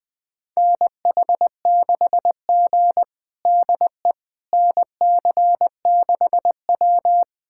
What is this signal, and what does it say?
Morse code, heard as a keyed tone, about 20 wpm, 700 Hz. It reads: NH6G DE NC6W
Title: USA Callsign DE Callsign - 4 Characters - 20wpm